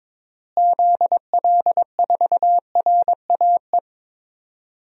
Morse code, heard as a keyed tone, about 22 wpm, 700 Hz. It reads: ZL4RAE
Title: DXCC-100 Call Signs - 6 Characters - Repeated 3x - 22wpm